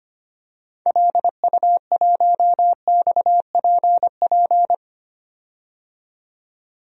Morse code, heard as a keyed tone, about 25 wpm, 690 Hz. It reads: LU1XPP
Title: DXCC-100 Call Signs - 6 Characters - 25wpm